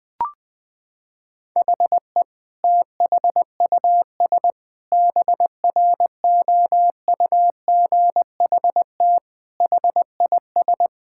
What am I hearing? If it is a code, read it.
Morse code, heard as a keyed tone, about 20 wpm, 700 Hz. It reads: HE THUS BROUGHT HIS
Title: Sentences from Top 1000 Words - Review Part 5 - 20wpm